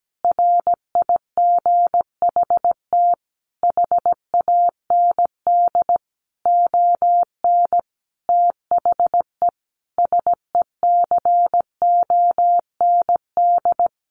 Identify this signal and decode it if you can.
Morse code, heard as a keyed tone, about 17 wpm, 700 Hz. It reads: RIGHT HAND ON THE SECOND
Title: Easy Phrases 17wpm